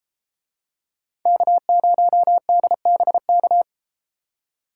Morse code, heard as a keyed tone, about 33 wpm, 700 Hz. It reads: K0DBK